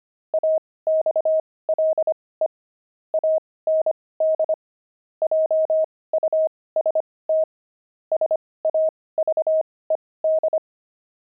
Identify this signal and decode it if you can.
Morse code, heard as a keyed tone, about 25 wpm, spaced as for 18 wpm, 630 Hz. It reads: AXLE AND JUST SAVED